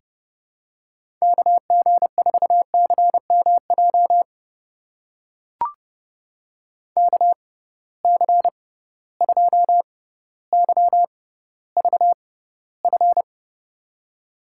Morse code, heard as a keyed tone, about 31 wpm, spaced as for 16 wpm, 700 Hz. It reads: KG4CMJ KC2YVF
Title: USA Call Signs - 6 Characters - 6x Character Spacing - 30wpm